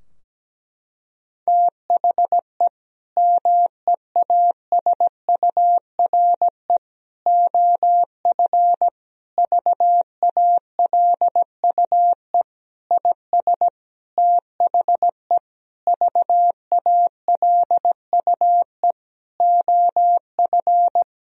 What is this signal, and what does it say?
Morse code, heard as a keyed tone, about 17 wpm, 700 Hz. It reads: THE MEASURE OF VALUE IS THE VALUE OF
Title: Sentences from Top 1000 Words - Review Part 3 - 17wpm